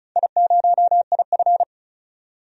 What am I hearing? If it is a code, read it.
Morse code, heard as a keyed tone, about 36 wpm, 700 Hz. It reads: I0IF
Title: DXCC-100 Call Signs - 4 Characters - 35wpm